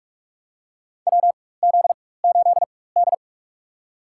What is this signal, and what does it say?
Morse code, heard as a keyed tone, about 46 wpm, spaced as for 21 wpm, 700 Hz. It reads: WZ8D
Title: USA Call Signs - 4x Character Spacing - 45wpm